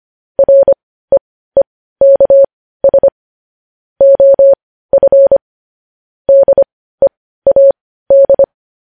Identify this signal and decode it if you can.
Morse code, heard as a keyed tone, about 25 wpm, spaced as for 15 wpm, 550 Hz. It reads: REEKS OF DEAD
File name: podcast / News headlines-1x-25x15